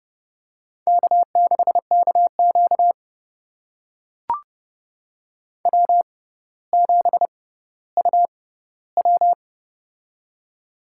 Morse code, heard as a keyed tone, about 31 wpm, spaced as for 15 wpm, 700 Hz. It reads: K6KQ W7UW
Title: USA Call Signs - 4 Characters - 6x Character Spacing - 30wpm